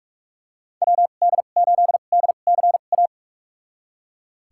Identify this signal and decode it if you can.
Morse code, heard as a keyed tone, about 42 wpm, spaced as for 31 wpm, 700 Hz. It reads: WD8DCA
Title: USA Call Signs - 6 Characters - 2x Character Spacing - 45wpm